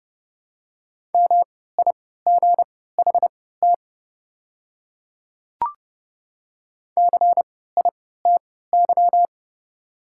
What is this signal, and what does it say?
Morse code, heard as a keyed tone, about 30 wpm, spaced as for 17 wpm, 700 Hz. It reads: MIGHT CITY
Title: 3x Character Spacing - Top 300 Words - 30wpm